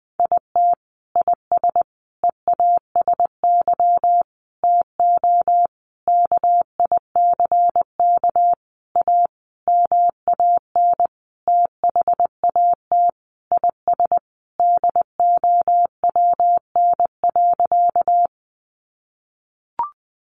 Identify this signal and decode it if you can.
Morse code, heard as a keyed tone, about 20 wpm, 700 Hz. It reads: IT IS EASY TO KICK A MAN THAT IS DOWN.
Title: Sentences from Top 1300 Words 20wpm